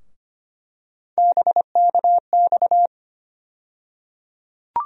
Morse code, heard as a keyed tone, about 25 wpm, 700 Hz. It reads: BKX
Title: USA Call Sign Postfixes - 3 Letters 25wpm